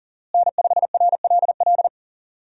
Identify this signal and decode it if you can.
Morse code, heard as a keyed tone, about 40 wpm, 700 Hz. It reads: N5RLL
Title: USA Call Signs - 5 Characters - Repeated 2x - 40wpm